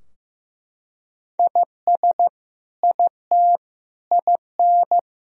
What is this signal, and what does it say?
Morse code, heard as a keyed tone, about 15 wpm, 700 Hz. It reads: IS IT IN
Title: Sentences from Top 1000 Words - Review Part 4 - 15wpm